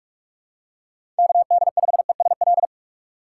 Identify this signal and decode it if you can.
Morse code, heard as a keyed tone, about 44 wpm, 700 Hz. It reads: KD5ESL
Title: USA Call Signs - 6 Characters - 4x Character Spacing - 45wpm